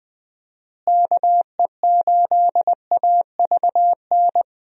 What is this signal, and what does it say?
Morse code, heard as a keyed tone, about 20 wpm, 700 Hz. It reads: KE8AVN
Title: USA Call Signs - 6 Characters - Repeated 3x - 20wpm